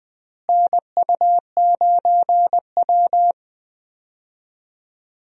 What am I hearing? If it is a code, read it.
Morse code, heard as a keyed tone, about 20 wpm, 690 Hz. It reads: NU9W
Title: USA Call Signs - 4 Characters - Repeated 3x - 20wpm